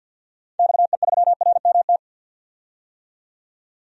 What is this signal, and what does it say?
Morse code, heard as a keyed tone, about 52 wpm, 700 Hz. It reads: XE3RMT
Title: DXCC-100 Call Signs - 6 Characters - 50wpm